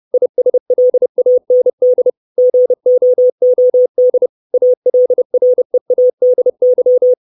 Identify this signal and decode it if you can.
Morse code, heard as a keyed tone, about 30 wpm, 500 Hz. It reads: ISLAND GOOD ALREADY
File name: 632 Words 30WPM